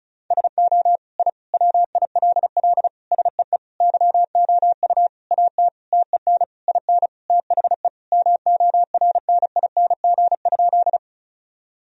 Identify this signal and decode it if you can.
Morse code, heard as a keyed tone, about 34 wpm, 700 Hz. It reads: SO I WILL SEE YOU AT TEN IN THE MORNING?
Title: Sentences from Top 1000 Words - Review Part 2 - 35wpm